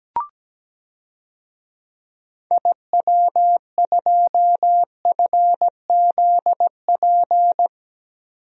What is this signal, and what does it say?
Morse code, heard as a keyed tone, about 17 wpm, 700 Hz. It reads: IW2FZP
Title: DXCC-100 Call Signs - 6 Characters - 17wpm